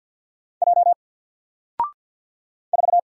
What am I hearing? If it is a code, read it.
Morse code, heard as a keyed tone, about 50 wpm, 700 Hz. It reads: J 4
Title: Single Character (Letter:Number:Punctuation:Prosign) 50wpm